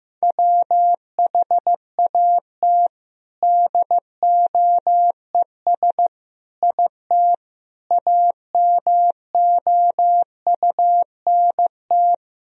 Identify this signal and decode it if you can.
Morse code, heard as a keyed tone, about 15 wpm, 700 Hz. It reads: WHAT DOES IT AMOUNT
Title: Sentences from Top 1000 Words - Review Part 4 - 15wpm